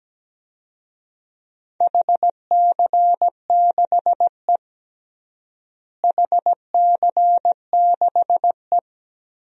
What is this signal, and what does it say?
Morse code, heard as a keyed tone, about 17 wpm, 700 Hz. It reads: HC6E HC6E
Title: DXCC-100 Call Signs - 4 Characters - Repeated 3x - 17wpm